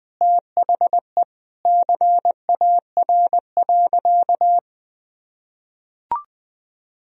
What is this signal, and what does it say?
Morse code, heard as a keyed tone, about 20 wpm, 700 Hz. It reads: THE CAR.
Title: Sentences from Top 1000 Words - Review Part 1 - 20wpm